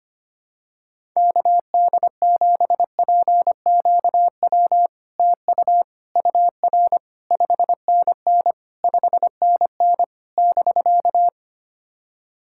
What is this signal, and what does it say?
Morse code, heard as a keyed tone, about 25 wpm, 700 Hz. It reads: KD7PQW TU UR 5NN 5NN <BK>
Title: SOTA QSO Practice 25wpm